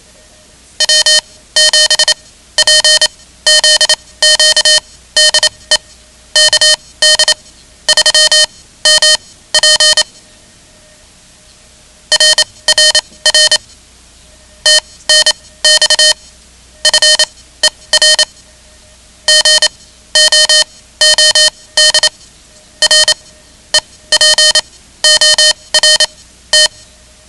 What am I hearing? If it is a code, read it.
Morse code, heard as a keyed tone, about 27 wpm, spaced as for 16 wpm, 3050 Hz. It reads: W7PZQ DE KD3MP RRR TNX FER GOOD REPORT
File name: ham radio / CodeQuick / Code Quick 10